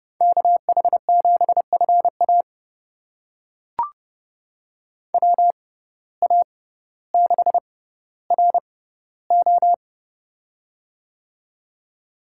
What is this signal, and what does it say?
Morse code, heard as a keyed tone, about 32 wpm, spaced as for 16 wpm, 700 Hz. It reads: KH7FA WA6RO
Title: USA Call Signs - 5 Characters - 6x Character Spacing - 30wpm